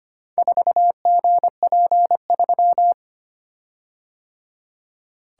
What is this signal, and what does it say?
Morse code, heard as a keyed tone, about 25 wpm, 700 Hz. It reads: 4GP3
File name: practice / 25wpm / Four Letter-Number Combinations - 25wpm